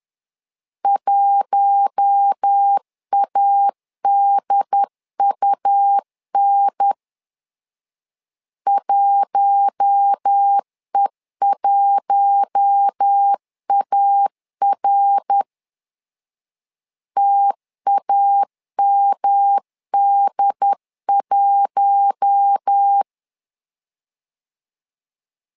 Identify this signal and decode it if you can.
Morse code, heard as a keyed tone, about 10 wpm, 800 Hz. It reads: 1ADUN 1E1AR TAMD1
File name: LEZIONI / 4 Q1 10wpm 2448